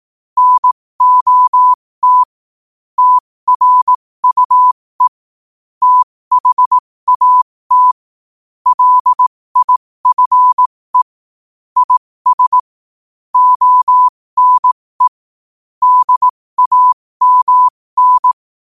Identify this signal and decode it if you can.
Morse code, heard as a keyed tone, about 18 wpm, spaced as for 15 wpm, 1000 Hz. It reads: NOT TRUE THAT LIFE IS ONE DAMN
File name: qotdfiles / 15wpm